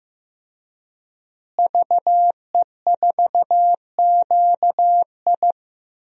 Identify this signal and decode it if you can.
Morse code, heard as a keyed tone, about 15 wpm, 700 Hz. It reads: VE4QI
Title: DXCC-100 Call Signs - 5 Characters - Repeated 3x - 15wpm